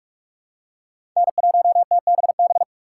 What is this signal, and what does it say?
Morse code, heard as a keyed tone, about 43 wpm, 700 Hz. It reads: N1TBB
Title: USA Call Signs - 4x Character Spacing - 45wpm